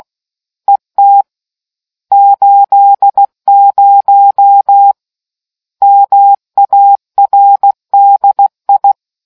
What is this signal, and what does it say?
Morse code, heard as a keyed tone, about 16 wpm, 800 Hz. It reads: ET 80 MARDI